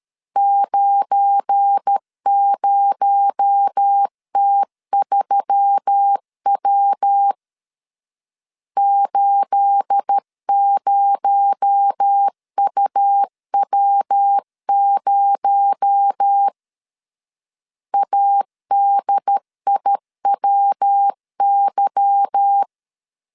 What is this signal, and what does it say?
Morse code, heard as a keyed tone, about 13 wpm, 800 Hz. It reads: 90T3W 80UW0 ADIWY
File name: CW / DET / 1 9 1500C S A 12wpm 3800